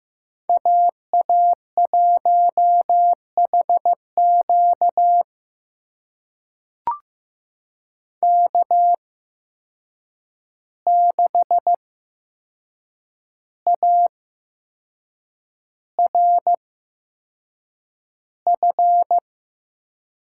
Morse code, heard as a keyed tone, about 15 wpm, spaced as for 6 wpm, 700 Hz. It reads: AA1HQ K6ARF
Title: USA Call Signs - 5 Characters - 8x Character Spacing - 15wpm